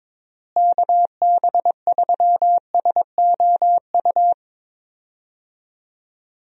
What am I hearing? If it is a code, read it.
Morse code, heard as a keyed tone, about 22 wpm, 700 Hz. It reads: KB3SOU
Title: USA Call Signs - 6 Characters - Repeated 3x - 22wpm